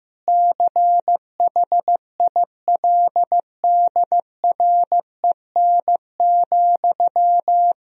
Morse code, heard as a keyed tone, about 15 wpm, 700 Hz. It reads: CHILDREN,
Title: Sentences from Top 1000 Words - Review Part 3 - 15wpm